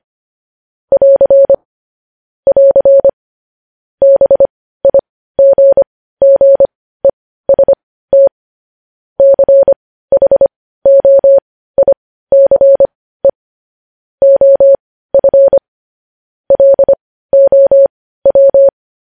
Morse code, heard as a keyed tone, about 25 wpm, spaced as for 15 wpm, 550 Hz. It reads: <AR> <AR> BIGGEST CHOICE OF LOW
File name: podcast / News headlines-1x-25x15